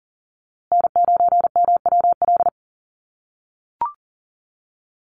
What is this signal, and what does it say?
Morse code, heard as a keyed tone, about 40 wpm, 700 Hz. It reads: N9MWL
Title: USA Call Signs Repeated 2x 40wpm